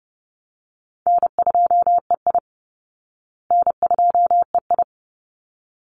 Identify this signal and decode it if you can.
Morse code, heard as a keyed tone, about 30 wpm, 700 Hz. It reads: N2EI N2EI
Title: Call Signs Repeated 3x - 30wpm